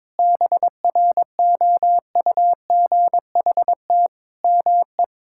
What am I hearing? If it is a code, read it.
Morse code, heard as a keyed tone, about 21 wpm, 700 Hz. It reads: BROUGHT ME
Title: Sentences from Top 1000 Words - Review Part 6 - 22wpm